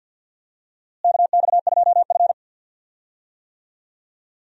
Morse code, heard as a keyed tone, about 48 wpm, 700 Hz. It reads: KX2F
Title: USA Call Signs - 4 Characters - Repeated 2x - 50wpm